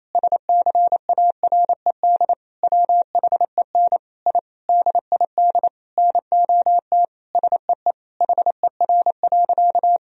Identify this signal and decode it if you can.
Morse code, heard as a keyed tone, about 28 wpm, 700 Hz. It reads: SCARED WHEN I DID NOT SEE HER.
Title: Sentences from Top 1000 Words - Review Part 4 - 28wpm